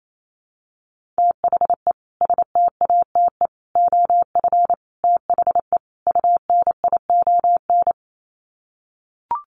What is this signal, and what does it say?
Morse code, heard as a keyed tone, about 28 wpm, 700 Hz. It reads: THE STATE OF THE UNION
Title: Sets of 5 Words Encore 28wpm